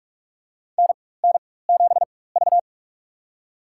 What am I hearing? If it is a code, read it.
Morse code, heard as a keyed tone, about 47 wpm, spaced as for 21 wpm, 700 Hz. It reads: NN7V
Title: USA Call Signs - 4x Character Spacing - 45wpm